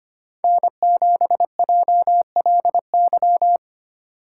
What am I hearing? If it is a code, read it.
Morse code, heard as a keyed tone, about 25 wpm, 700 Hz. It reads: N7JLY
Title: USA Call Signs - 5 Characters - 3x Character Spacing - 25wpm